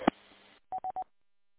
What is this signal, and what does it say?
There is no code.